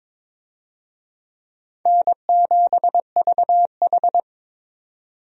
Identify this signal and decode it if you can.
Morse code, heard as a keyed tone, about 22 wpm, 700 Hz. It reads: N7VH
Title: USA Call Signs - 4 Characters - 22wpm